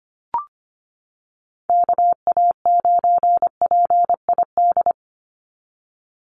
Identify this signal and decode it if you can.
Morse code, heard as a keyed tone, about 25 wpm, 700 Hz. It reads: KA9PID
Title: USA Call Signs 25wpm